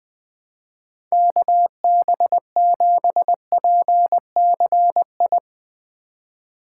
Morse code, heard as a keyed tone, about 20 wpm, 700 Hz. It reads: KB7PCI